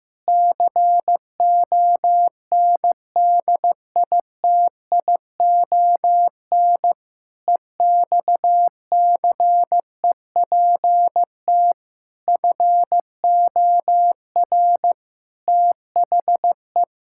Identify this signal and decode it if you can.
Morse code, heard as a keyed tone, about 15 wpm, 700 Hz. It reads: CONDITION EXCEPT FOR THE
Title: Pangram Challenge v2 - 15wpm